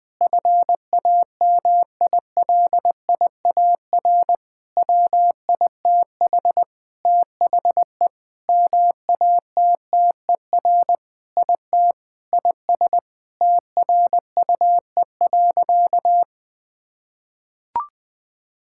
Morse code, heard as a keyed tone, about 20 wpm, 700 Hz. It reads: FAMILIAR WITH THE MATTER IT IS TRUE.
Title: Sentences with 7-Word Phrases - 20wpm